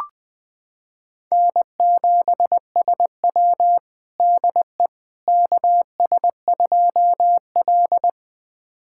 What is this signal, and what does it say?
Morse code, heard as a keyed tone, about 20 wpm, 700 Hz. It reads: N7SW DE KS2L
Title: USA Callsign DE Callsign - 4 Characters - 20wpm